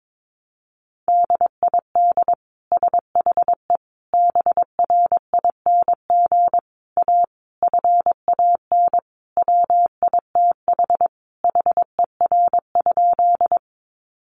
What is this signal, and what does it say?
Morse code, heard as a keyed tone, about 22 wpm, 700 Hz. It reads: DID SHE BRING A FAN WITH HER?
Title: Sentences from Top 1300 Words 22wpm